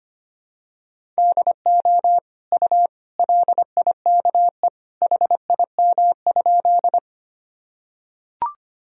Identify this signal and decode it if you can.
Morse code, heard as a keyed tone, about 25 wpm, 700 Hz. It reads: DO U LIKE HIM?
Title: Sentences from Top 100 Words and Abbreviations 25wpm